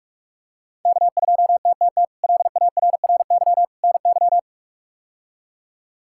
Morse code, heard as a keyed tone, about 46 wpm, 700 Hz. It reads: K2TTT LARRY NY
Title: K1USN SST QSO Practice - 45wpm